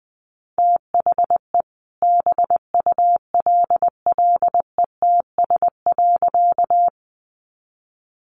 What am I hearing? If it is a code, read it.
Morse code, heard as a keyed tone, about 20 wpm, 700 Hz. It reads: THE BULLETS.
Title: A Sound Of Thunder 20wpm